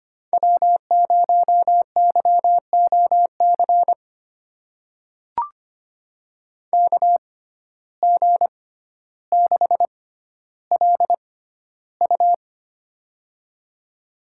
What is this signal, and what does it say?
Morse code, heard as a keyed tone, about 25 wpm, spaced as for 13 wpm, 700 Hz. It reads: W0YOC KG6LU
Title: USA Call Signs - 5 Characters - 6x Character Spacing - 25wpm